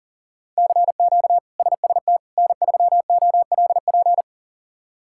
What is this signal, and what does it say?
Morse code, heard as a keyed tone, about 40 wpm, 700 Hz. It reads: CQ SST N3OLP
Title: K1USN SST QSO Practice - 40wpm